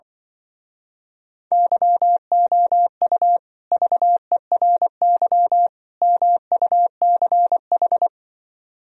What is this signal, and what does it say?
Morse code, heard as a keyed tone, about 24 wpm, 700 Hz. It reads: YOU VERY MUCH
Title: Speed-Racing - Sets of 3 Words 20wpm